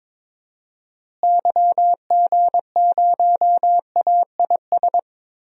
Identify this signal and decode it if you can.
Morse code, heard as a keyed tone, about 22 wpm, 700 Hz. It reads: YG0AIS
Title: DXCC-100 Call Signs - 6 Characters - Repeated 3x - 22wpm